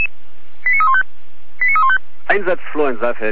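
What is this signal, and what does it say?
Five-tone selective call (ZVEI), heard as quick runs of five tones, tones of about 70 ms, with speech. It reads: 89315 89315
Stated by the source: ZVEI five-tone selective call received on SDR, fire brigade alerting.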